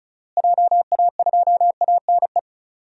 Morse code, heard as a keyed tone, about 36 wpm, 700 Hz. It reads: JA2ANE
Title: DXCC-100 Call Signs - 6 Characters - Repeated 3x - 35wpm